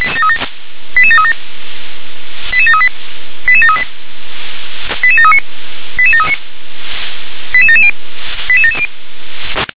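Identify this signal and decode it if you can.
Five-tone selective call (ZVEI), heard as quick runs of five tones, tones of about 70 ms. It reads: 88737 88738 88738 88739 88739 88770 88770
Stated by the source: ZVEI five-tone selective call 88740, received by SDR.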